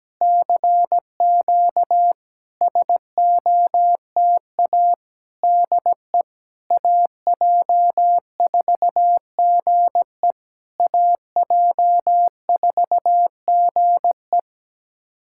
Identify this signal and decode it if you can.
Morse code, heard as a keyed tone, about 17 wpm, 690 Hz. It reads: CQ SOTA DE AJ4GE AJ4GE
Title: SOTA QSO Practice 17wpm